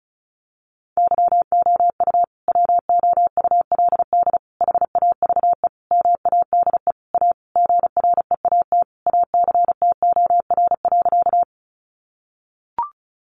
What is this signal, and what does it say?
Morse code, heard as a keyed tone, about 34 wpm, 700 Hz. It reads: YOU WOULD HAVE MADE A GREAT ACTOR.
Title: Sentences from Top 1500 Words 35wpm